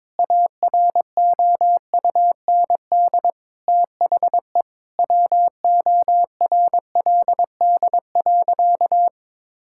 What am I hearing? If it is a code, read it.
Morse code, heard as a keyed tone, about 21 wpm, 700 Hz. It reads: AROUND THE WORLD.
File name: Sentences with 7-Word Phrases - 22wpm